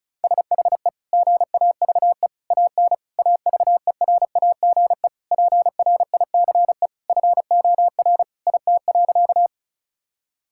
Morse code, heard as a keyed tone, about 34 wpm, 690 Hz. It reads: SHE GAVE AN AVERAGE PRICE FOR IT.
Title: Sentences from Top 1000 Words - Review Part 6 - 35wpm